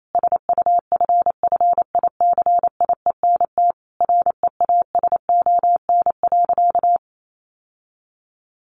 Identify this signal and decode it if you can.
Morse code, heard as a keyed tone, about 28 wpm, 700 Hz. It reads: SUFFICIENT REASON.
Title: Sentences from Top 1400 Words 28wpm